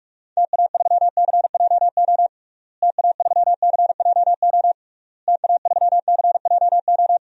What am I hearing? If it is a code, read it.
Morse code, heard as a keyed tone, about 46 wpm, 700 Hz. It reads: TA3CJO TA3CJO TA3CJO
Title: DXCC-100 Call Signs - 6 Characters - Repeated 3x - 45wpm